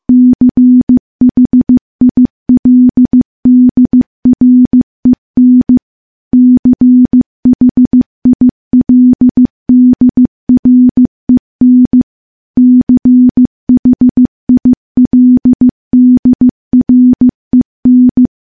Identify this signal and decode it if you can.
Morse code, heard as a keyed tone, about 15 wpm, 250 Hz. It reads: CHILDREN CHILDREN CHILDREN
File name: morse-training-15WPM